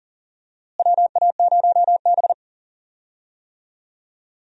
Morse code, heard as a keyed tone, about 37 wpm, 700 Hz. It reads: WA0B
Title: USA Call Signs - 4 Characters - Repeated 3x - 40wpm